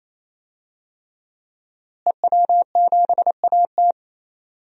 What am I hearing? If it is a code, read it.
Morse code, heard as a keyed tone, about 28 wpm, 700 Hz. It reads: EW7AT
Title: DXCC-100 Call Signs - 5 Characters - Repeated 3x - 28wpm